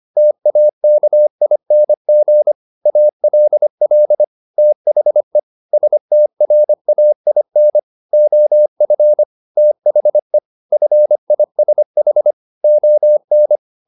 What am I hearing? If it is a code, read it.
Morse code, heard as a keyed tone, about 25 wpm, 600 Hz. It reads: TAKING ALL THE STRAIN OF THE FISH ON
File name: the old man and the sea part 1 25wpm